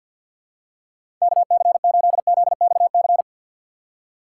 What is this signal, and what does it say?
Morse code, heard as a keyed tone, about 53 wpm, 700 Hz. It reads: KC8ZXC
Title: USA Call Signs - 6 Characters - 6x Character Spacing - 50wpm